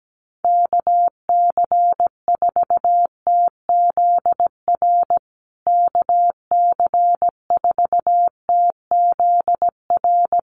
Morse code, heard as a keyed tone, about 17 wpm, 700 Hz. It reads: KC4TZR KC4TZR
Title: USA Call Signs Repeated 3x 17wpm